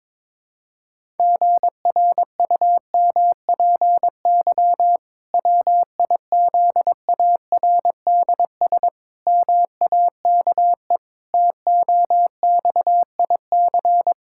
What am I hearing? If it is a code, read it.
Morse code, heard as a keyed tone, about 22 wpm, 700 Hz. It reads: GRUMPY WIZARDS MAKE TOXIC
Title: Pangram Challenge v2 - 22wpm